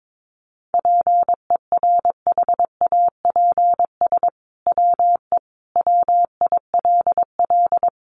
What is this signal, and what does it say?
Morse code, heard as a keyed tone, about 22 wpm, 700 Hz. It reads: PERHAPS WE WILL
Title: Sentences from Top 1100 Words 22wpm